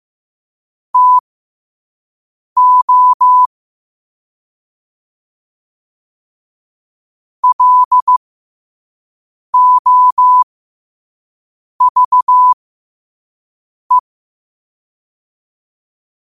Morse code, heard as a keyed tone, about 15 wpm, spaced as for 5 wpm, 1000 Hz. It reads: TO LOVE
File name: qotdfiles / 5wpm